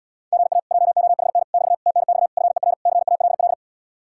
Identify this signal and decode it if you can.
Morse code, heard as a keyed tone, about 80 wpm, 700 Hz. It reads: GA JOHN <BT> TNX FER CALL